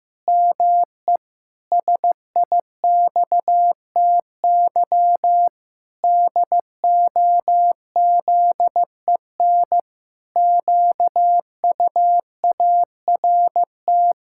Morse code, heard as a keyed tone, about 15 wpm, 700 Hz. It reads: ME SIXTY DOZEN QUART
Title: Pangram Challenge v2 - 15wpm